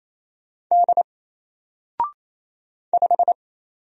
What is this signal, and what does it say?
Morse code, heard as a keyed tone, about 28 wpm, 700 Hz. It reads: D 5
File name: Single-Letter-Number-28wpm